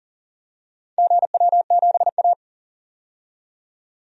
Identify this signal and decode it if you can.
Morse code, heard as a keyed tone, about 40 wpm, 700 Hz. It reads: GW7A
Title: DXCC-100 Call Signs - 4 Characters - 40wpm